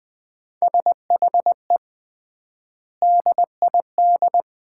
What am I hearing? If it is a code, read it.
Morse code, heard as a keyed tone, about 20 wpm, 700 Hz. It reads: SHE DID
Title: 3x Word-Spacing - Sets of 4 Words - 20wpm